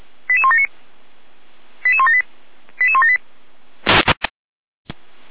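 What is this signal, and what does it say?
Five-tone selective call (ZVEI), heard as quick runs of five tones, tones of about 70 ms. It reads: 88179 88178 88178
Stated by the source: ZVEI five-tone selective call 88171 received by SDR.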